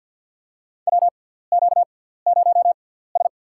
Morse code, heard as a keyed tone, about 50 wpm, spaced as for 17 wpm, 690 Hz. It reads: WQ0S
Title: USA Call Signs - 4 Characters - 6x Character Spacing - 50wpm